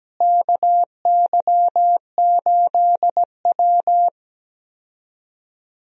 Morse code, heard as a keyed tone, about 17 wpm, 700 Hz. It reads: KY8W
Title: USA Call Signs - 4 Characters - Repeated 3x - 17wpm